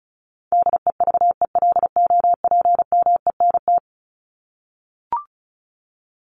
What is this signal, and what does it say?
Morse code, heard as a keyed tone, about 34 wpm, 700 Hz. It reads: DEVELOPMENT
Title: Top 501-750 Words 35wpm